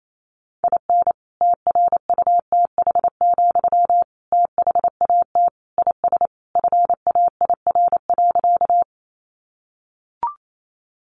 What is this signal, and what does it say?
Morse code, heard as a keyed tone, about 28 wpm, 700 Hz. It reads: IN TRUTH, THAT IS FAIR.
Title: Sentences from Top 1300 Words 28wpm